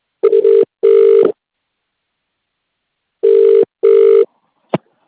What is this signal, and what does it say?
Ringback tone (UK), 400 + 450 Hz.